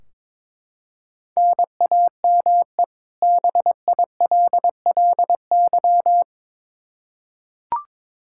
Morse code, heard as a keyed tone, about 22 wpm, 700 Hz. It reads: NAME BILLY
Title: Practice QSO - Exchange 1 - Beginning - 22wpm